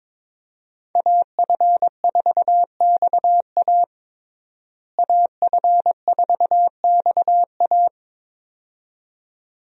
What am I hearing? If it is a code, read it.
Morse code, heard as a keyed tone, about 22 wpm, 700 Hz. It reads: AF4XA AF4XA
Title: USA Call Signs - 5 Characters - Repeated 2x - 22wpm